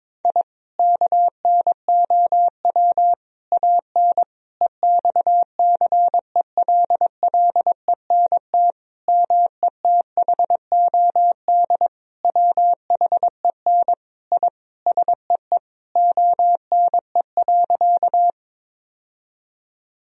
Morse code, heard as a keyed tone, about 22 wpm, 700 Hz. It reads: I KNOW AN EXCELLENT METHOD WHEN I SEE ONE.
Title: Sentences from Top 1000 Words - Review Part 3 - 22wpm